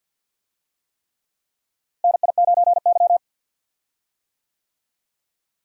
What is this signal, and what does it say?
Morse code, heard as a keyed tone, about 52 wpm, 700 Hz. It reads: NI9Y
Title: USA Call Signs - 4 Characters - Repeated 3x - 50wpm